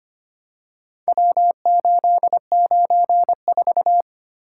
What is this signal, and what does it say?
Morse code, heard as a keyed tone, about 25 wpm, 700 Hz. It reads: W894
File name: practice / 25wpm / Four Letter-Number Combinations - 25wpm